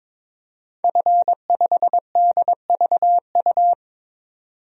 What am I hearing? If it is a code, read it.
Morse code, heard as a keyed tone, about 22 wpm, 700 Hz. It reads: F5DVU
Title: DXCC-100 Call Signs - 5 Characters - 22wpm